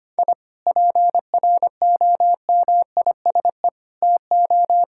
Morse code, heard as a keyed tone, about 25 wpm, 700 Hz. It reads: I PROMISE TO
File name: practice / 25wpm / Difficult Sentences with Hints - 25wpm